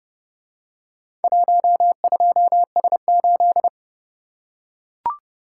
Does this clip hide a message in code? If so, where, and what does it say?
Yes: Morse code from 1.2 to 3.7 s, heard as a keyed tone, about 30 wpm, 700 Hz. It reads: 12S8